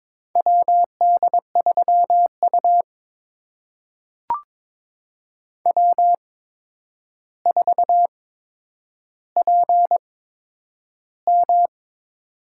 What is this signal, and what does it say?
Morse code, heard as a keyed tone, about 22 wpm, spaced as for 9 wpm, 700 Hz. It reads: WD3U W4PM
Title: USA Call Signs - 4 Characters - 8x Character Spacing - 22wpm